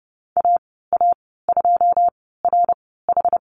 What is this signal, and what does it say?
Morse code, heard as a keyed tone, about 30 wpm, spaced as for 17 wpm, 700 Hz. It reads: AA2RH
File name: Lesson 15g - Number 5 - Call Signs 30wpm